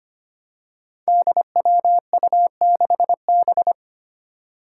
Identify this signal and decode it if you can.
Morse code, heard as a keyed tone, about 25 wpm, 700 Hz. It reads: DWU6B
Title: Five Challenging Letter-Number Combinations 25wpm